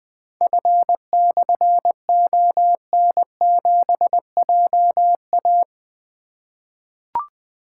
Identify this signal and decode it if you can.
Morse code, heard as a keyed tone, about 20 wpm, 700 Hz. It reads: F/ON7JA
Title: Difficult Call Signs 20wpm